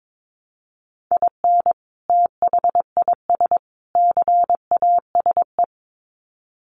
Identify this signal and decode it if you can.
Morse code, heard as a keyed tone, about 22 wpm, 700 Hz. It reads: IN THIS CASE
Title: Sets of 3 Words 22wpm